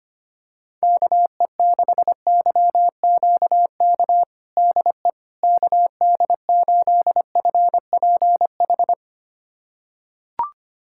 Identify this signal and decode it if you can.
Morse code, heard as a keyed tone, about 25 wpm, 700 Hz. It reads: KE6YQK DE KD8FPH
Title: USA Callsign DE Callsign 25wpm